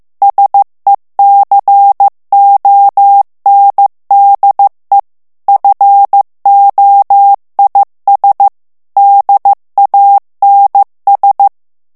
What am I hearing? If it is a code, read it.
Morse code, heard as a keyed tone, about 15 wpm, 800 Hz. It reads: SECONDE FOIS DANS
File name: morse mp3 / FRANCAIS TEXTE CLAIR 2 VITESSE 14 MOTS MINUTES